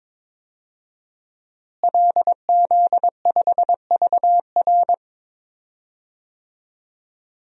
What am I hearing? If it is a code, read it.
Morse code, heard as a keyed tone, about 22 wpm, 700 Hz. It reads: LZ5VR
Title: DXCC-100 Call Signs - 5 Characters - Repeated 2x - 22wpm